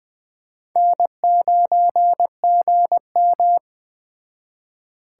Morse code, heard as a keyed tone, about 20 wpm, 700 Hz. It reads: N9GM